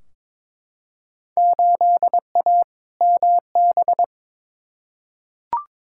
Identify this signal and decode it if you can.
Morse code, heard as a keyed tone, about 22 wpm, 700 Hz. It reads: 8A MB
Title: Field Day Exchanges Encore - 22wpm